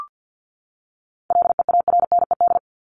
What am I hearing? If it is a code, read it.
Morse code, heard as a keyed tone, about 50 wpm, 700 Hz. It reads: LEARNED
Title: ED Suffix - Top 100 Words with Base Words 50wpm